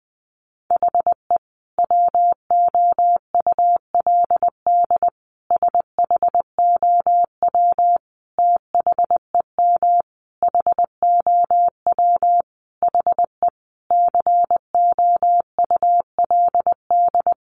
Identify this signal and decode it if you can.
Morse code, heard as a keyed tone, about 20 wpm, 700 Hz. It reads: HE WOULD SHOW THEM HOW HE COULD
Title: Sentences from Top 1300 Words 20wpm